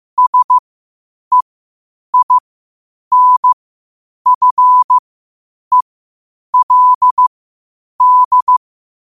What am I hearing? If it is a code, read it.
Morse code, heard as a keyed tone, about 15 wpm, spaced as for 8 wpm, 1000 Hz. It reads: SEINFELD
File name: qotdfiles / 8wpm